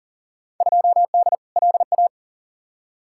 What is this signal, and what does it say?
Morse code, heard as a keyed tone, about 40 wpm, 700 Hz. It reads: 2D LA
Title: Field Day Exchanges - 40wpm